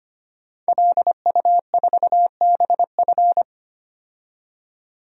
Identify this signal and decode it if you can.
Morse code, heard as a keyed tone, about 25 wpm, 700 Hz. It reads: LU4BF
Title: DXCC-100 Call Signs - 5 Characters - 25wpm